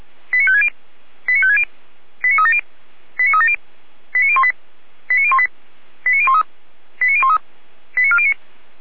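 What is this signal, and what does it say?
Five-tone selective call (ZVEI), heard as quick runs of five tones, tones of about 70 ms. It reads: 89560 89560 89370 89370 89018 89018 89013 89013 89409
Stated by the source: ZVEI five-tone call received by SDR, from feuerwehr-orlamuende.org (fire brigade Orlamünde).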